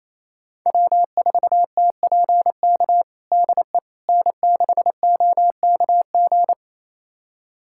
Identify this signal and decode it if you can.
Morse code, heard as a keyed tone, about 28 wpm, 700 Hz. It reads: W4TPK DE N6OKG
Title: USA Callsign DE Callsign - 5 Characters - 28wpm